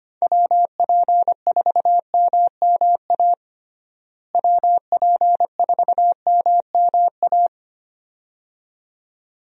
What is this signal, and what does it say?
Morse code, heard as a keyed tone, about 25 wpm, 700 Hz. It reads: WP4MMA WP4MMA
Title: USA Call Signs - 6 Characters - Repeated 2x - 25wpm